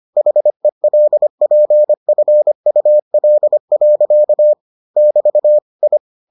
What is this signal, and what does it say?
Morse code, heard as a keyed tone, about 25 wpm, 600 Hz. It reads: HELPFUL. <BT> I